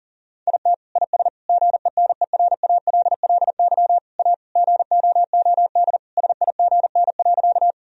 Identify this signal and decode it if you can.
Morse code, heard as a keyed tone, about 40 wpm, 700 Hz. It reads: IT IS GENERALLY A GOOD SIGN.
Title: Sentences from Top 1000 Words - Review Part 4 - 40wpm